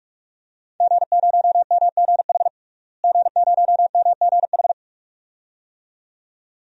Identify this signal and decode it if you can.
Morse code, heard as a keyed tone, about 46 wpm, 700 Hz. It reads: G0MGH G0MGH
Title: DXCC-100 Call Signs - 5 Characters - Repeated 2x - 45wpm